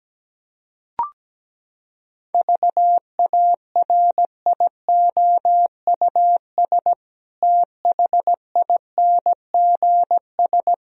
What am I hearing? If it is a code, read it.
Morse code, heard as a keyed tone, about 17 wpm, 700 Hz. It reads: VARIOUS THINGS
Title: Sentences from Top 1000 Words - Review Part 4 - 17wpm